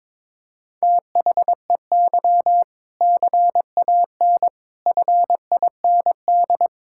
Morse code, heard as a keyed tone, about 22 wpm, 700 Hz. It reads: THEY CAN FIND
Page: morse-code-ninja-practice-22wpm